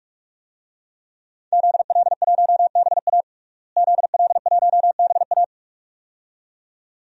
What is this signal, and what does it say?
Morse code, heard as a keyed tone, about 44 wpm, 700 Hz. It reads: ZL1BA ZL1BA
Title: DXCC-100 Call Signs - 5 Characters - Repeated 2x - 45wpm